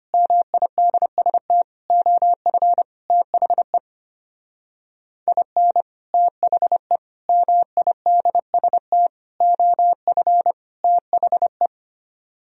Morse code, heard as a keyed tone, about 30 wpm, 700 Hz. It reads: MIDST OF THE IN THE MIDST OF THE
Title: Speed-Racing - Sets of 5 Words 25wpm